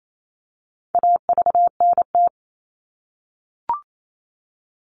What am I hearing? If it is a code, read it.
Morse code, heard as a keyed tone, about 28 wpm, 700 Hz. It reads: AVNT
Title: All Cut Numbers - 4-Digit 28wpm